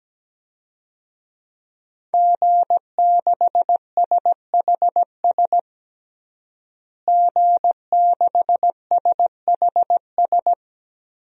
Morse code, heard as a keyed tone, about 17 wpm, 700 Hz. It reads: G6SHS G6SHS
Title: DXCC-100 Call Signs - 5 Characters - Repeated 3x - 17wpm